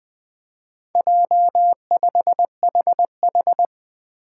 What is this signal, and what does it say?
Morse code, heard as a keyed tone, about 20 wpm, 700 Hz. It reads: J5HH
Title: Four Challenging Letter-Number Combinations 20wpm